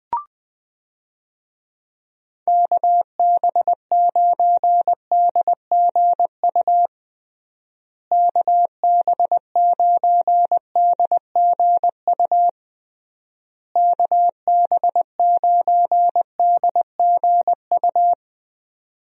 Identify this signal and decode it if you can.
Morse code, heard as a keyed tone, about 21 wpm, 700 Hz. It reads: KB9DGU KB9DGU KB9DGU
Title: USA Call Signs - 6 Characters - Repeated 3x - 20wpm